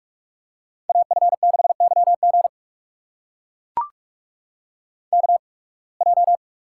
Morse code, heard as a keyed tone, about 44 wpm, spaced as for 29 wpm, 700 Hz. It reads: AF6YG KJ
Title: USA Call Signs - 8x Character Spacing - 45wpm